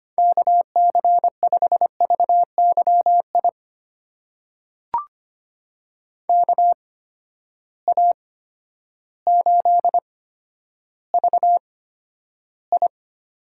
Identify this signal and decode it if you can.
Morse code, heard as a keyed tone, about 25 wpm, spaced as for 11 wpm, 700 Hz. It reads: KC5VYI KA8VI
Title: USA Call Signs - 6 Characters - 8x Character Spacing - 25wpm